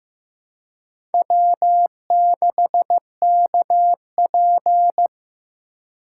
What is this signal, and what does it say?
Morse code, heard as a keyed tone, about 15 wpm, 700 Hz. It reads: W6KP
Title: USA Call Signs - 4 Characters - 3x Character Spacing - 15wpm